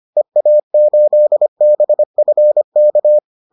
Morse code, heard as a keyed tone, about 25 wpm, 600 Hz. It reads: EA8BFK